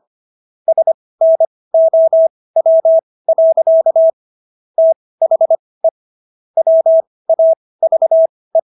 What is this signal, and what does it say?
Morse code, heard as a keyed tone, about 25 wpm, spaced as for 18 wpm, 650 Hz. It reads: SNOW. THE WAVE